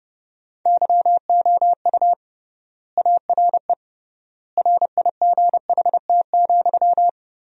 Morse code, heard as a keyed tone, about 30 wpm, 700 Hz. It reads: YOU ARE RIGHT,
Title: 3x Word-Spacing - Sentences from Top 400 Words - 30wpm